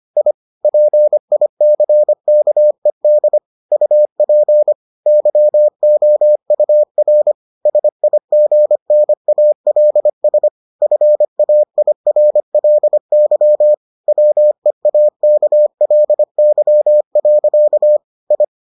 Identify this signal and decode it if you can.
Morse code, heard as a keyed tone, about 25 wpm, 600 Hz. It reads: I PICKED UP YOUR SIGNALS FAIRLY WEAKLY. I